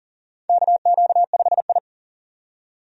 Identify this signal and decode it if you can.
Morse code, heard as a keyed tone, about 40 wpm, 700 Hz. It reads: KQ5I
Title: USA Call Signs - 4 Characters - 40wpm